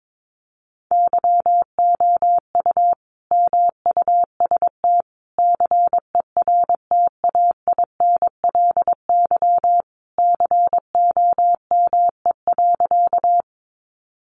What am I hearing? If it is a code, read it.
Morse code, heard as a keyed tone, about 22 wpm, 700 Hz. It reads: YOU MUST CERTAINLY COME.
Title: Sentences from Top 900 Words 22wpm